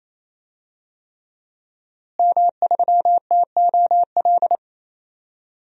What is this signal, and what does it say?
Morse code, heard as a keyed tone, about 28 wpm, 700 Hz. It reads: M3TOL
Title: DXCC-100 Call Signs - 5 Characters - Repeated 3x - 28wpm